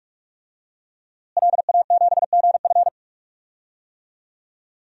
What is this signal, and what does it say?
Morse code, heard as a keyed tone, about 46 wpm, 700 Hz. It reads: LA7GF
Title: DXCC-100 Call Signs - 5 Characters - Repeated 3x - 45wpm